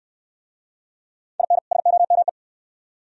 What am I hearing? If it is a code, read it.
Morse code, heard as a keyed tone, about 69 wpm, 700 Hz. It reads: IN SOME